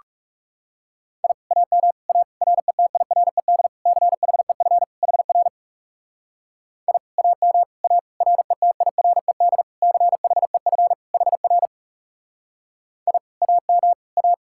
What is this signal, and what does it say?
Morse code, heard as a keyed tone, about 37 wpm, 700 Hz. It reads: I AM A RETIRED CHEF HR I AM A RETIRED CHEF HR I AM A
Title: Speed-Racing Common QSO Phrases - 30wpm